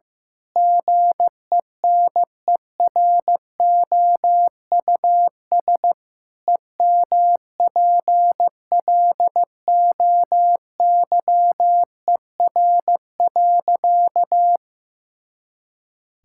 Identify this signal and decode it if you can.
Morse code, heard as a keyed tone, about 15 wpm, 700 Hz. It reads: GENEROUS EMPLOYER.
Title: Pangram Challenge v2 - 15wpm